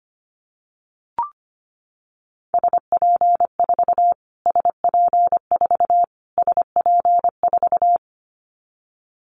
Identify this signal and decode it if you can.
Morse code, heard as a keyed tone, about 25 wpm, 700 Hz. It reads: SP4 SP4 SP4
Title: Callsign Prefixes for Top 25 DXCC - Repeated 3x - 25wpm